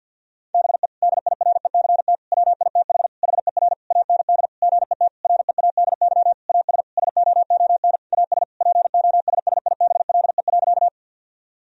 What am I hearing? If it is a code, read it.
Morse code, heard as a keyed tone, about 50 wpm, 700 Hz. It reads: BE DIRECT WITH HER AND GET READY AS SOON AS POSSIBLE.
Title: Sentences from Top 1000 Words - Review Part 2 - 50wpm